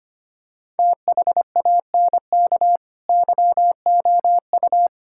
Morse code, heard as a keyed tone, about 25 wpm, 700 Hz. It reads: THANK YOU
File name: Sets of 9 Words - 25wpm